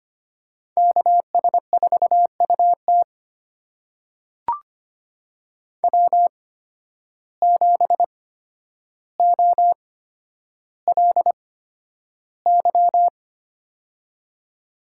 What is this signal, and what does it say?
Morse code, heard as a keyed tone, about 25 wpm, spaced as for 11 wpm, 700 Hz. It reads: KS4UT W7OLY
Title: USA Call Signs - 8x Character Spacing - 25wpm